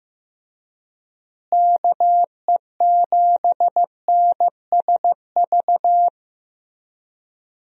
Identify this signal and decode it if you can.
Morse code, heard as a keyed tone, about 15 wpm, 700 Hz. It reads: KE7NSV
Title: USA Call Signs - 6 Characters - Repeated 2x - 15wpm